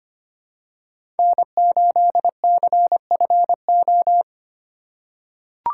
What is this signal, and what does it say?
Morse code, heard as a keyed tone, about 25 wpm, 700 Hz. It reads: N8CFO
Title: USA Call Signs - 5 Characters - 4x Character Spacing - 25wpm